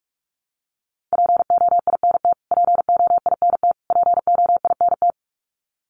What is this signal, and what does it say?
Morse code, heard as a keyed tone, about 44 wpm, 700 Hz. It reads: POINT POINT POINT
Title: Top 500 Words Repeated 45wpm